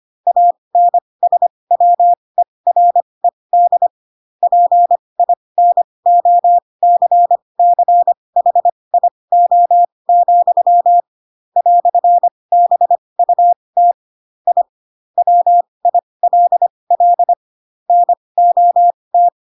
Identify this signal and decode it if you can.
Morse code, heard as a keyed tone, about 25 wpm, spaced as for 20 wpm, 700 Hz. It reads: ANSWERED PINOCCHIO, "BUT I WILL NOT
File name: Pinocchio / Pinocchio ch7 15 25 ik1hge